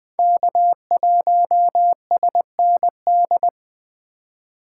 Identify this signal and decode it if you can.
Morse code, heard as a keyed tone, about 21 wpm, 700 Hz. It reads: K1SND
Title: K1USN SST QSO Practice - 20wpm